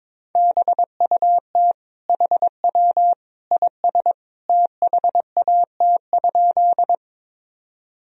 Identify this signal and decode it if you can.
Morse code, heard as a keyed tone, about 21 wpm, 700 Hz. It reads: BUT HW IS THAT?
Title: Sentences from Top 100 Words and Abbreviations 22wpm